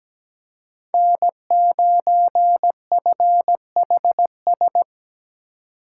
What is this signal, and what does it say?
Morse code, heard as a keyed tone, about 17 wpm, 700 Hz. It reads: N9FHS
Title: USA Call Signs - 2x Character Spacing - 17wpm